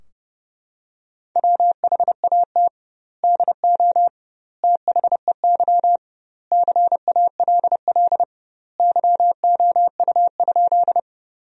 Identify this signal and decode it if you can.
Morse code, heard as a keyed tone, about 30 wpm, 700 Hz. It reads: WHAT DO THEY CALL YOU?
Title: 2x Word-Spacing - Sentences from Top 100 Words - 30wpm